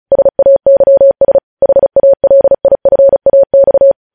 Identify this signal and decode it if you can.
Morse code, heard as a keyed tone, about 36 wpm, 550 Hz. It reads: SAYS HALIFAX